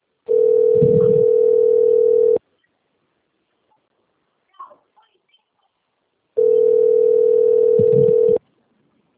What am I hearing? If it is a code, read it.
Ringback tone (North America), 440 + 480 Hz.